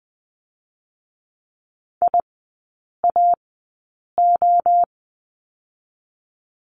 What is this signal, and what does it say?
Morse code, heard as a keyed tone, about 20 wpm, 700 Hz. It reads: I A O
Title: 029 Lesson 6e - Letter i - Three Characters 20wpm